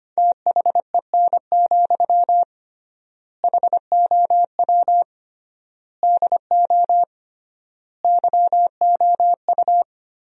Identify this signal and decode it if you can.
Morse code, heard as a keyed tone, about 25 wpm, 700 Hz. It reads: THEN, HOW DO YOU
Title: 3x Word-Spacing - Sentences from Top 100 Words - 25wpm